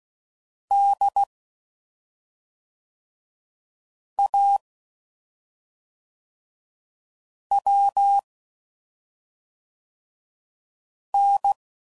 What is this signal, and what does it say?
Morse code, heard as a keyed tone, about 16 wpm, 800 Hz. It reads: D A W N
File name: Suffixes / Special ING's / SlowToFaster ING's